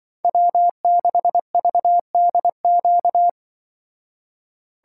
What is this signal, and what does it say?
Morse code, heard as a keyed tone, about 24 wpm, 700 Hz. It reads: W6VDQ
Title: Speed Racing - USA 1x3 Call Signs 20wpm